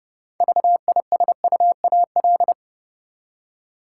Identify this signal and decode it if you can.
Morse code, heard as a keyed tone, about 29 wpm, 700 Hz. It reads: VISUAL
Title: Six Letter Words - 30wpm